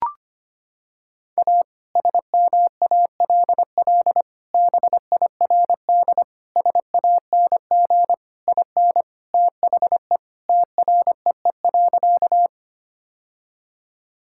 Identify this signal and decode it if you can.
Morse code, heard as a keyed tone, about 25 wpm, 700 Hz. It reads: A SMALL BIRD SANG IN THE TREE.
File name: practice / 25wpm / Sentences with 2-Word Phrases - 25wpm